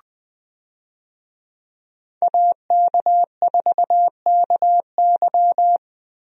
Morse code, heard as a keyed tone, about 20 wpm, 700 Hz. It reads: AK4KY
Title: USA Call Signs - 5 Characters - Repeated 3x - 20wpm